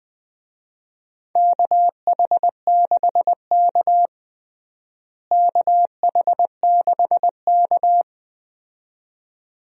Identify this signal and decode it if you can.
Morse code, heard as a keyed tone, about 20 wpm, 700 Hz. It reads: KH6K KH6K
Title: DXCC-100 Call Signs - 4 Characters - Repeated 2x - 20wpm